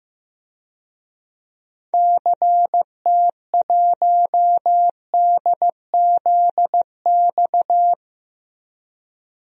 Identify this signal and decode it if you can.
Morse code, heard as a keyed tone, about 15 wpm, 700 Hz. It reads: CT1DZX
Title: DXCC-100 Call Signs - 6 Characters - 15wpm